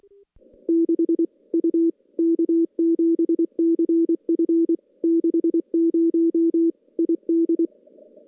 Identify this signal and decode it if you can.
Morse code, heard as a keyed tone, about 24 wpm, 330 Hz. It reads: 6 U K7CF 60 ID
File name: contests / CW-SS-k5tr- -2300